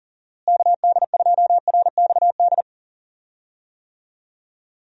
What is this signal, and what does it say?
Morse code, heard as a keyed tone, about 40 wpm, 700 Hz. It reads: KD2RXD
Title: USA Call Signs - 6 Characters - Repeated 2x - 40wpm